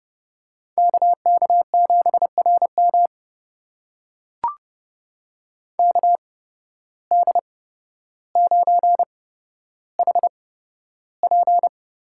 Morse code, heard as a keyed tone, about 32 wpm, spaced as for 13 wpm, 700 Hz. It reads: KK7RM KD9HP
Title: USA Call Signs - 8x Character Spacing - 30wpm